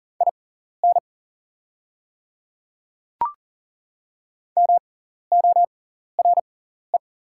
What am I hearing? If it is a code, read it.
Morse code, heard as a keyed tone, about 40 wpm, spaced as for 14 wpm, 700 Hz. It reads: IN MORE
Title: 6x Character Spacing - Top 200 Words - 40wpm